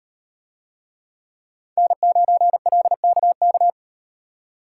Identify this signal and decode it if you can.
Morse code, heard as a keyed tone, about 38 wpm, 700 Hz. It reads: N9LKK